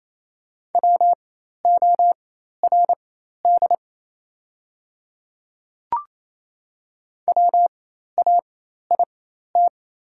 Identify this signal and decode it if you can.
Morse code, heard as a keyed tone, about 27 wpm, spaced as for 13 wpm, 690 Hz. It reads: WORD WAIT
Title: 4x Character Spacing - Top 500 Words - 28wpm